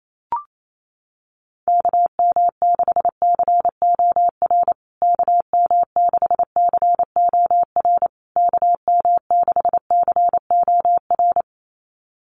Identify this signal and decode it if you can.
Morse code, heard as a keyed tone, about 28 wpm, 700 Hz. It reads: KM6COR KM6COR KM6COR
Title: USA Call Signs Repeated 3x 28wpm